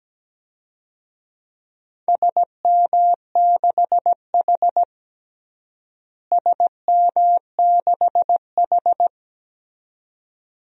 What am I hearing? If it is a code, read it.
Morse code, heard as a keyed tone, about 17 wpm, 700 Hz. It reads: SM6H SM6H
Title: DXCC-100 Call Signs - 4 Characters - Repeated 2x - 17wpm